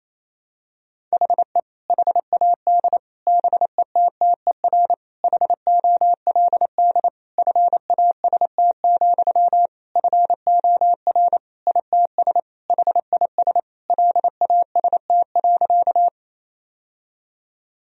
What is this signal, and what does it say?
Morse code, heard as a keyed tone, about 28 wpm, 700 Hz. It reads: HE HAD BETTER HOLD FAST, FOR ITS HIS LAST.
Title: Sentences from Top 1000 Words - Review Part 2 - 28wpm